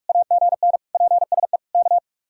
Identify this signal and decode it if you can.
Morse code, heard as a keyed tone, about 41 wpm, 700 Hz. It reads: AGN PSE K